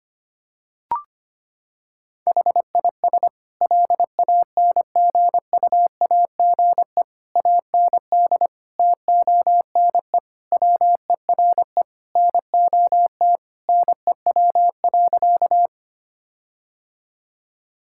Morse code, heard as a keyed tone, about 25 wpm, 700 Hz. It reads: HIS LANGUAGE AND TONE WERE NOT NEW.
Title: Sentences from Top 1600 Words 25wpm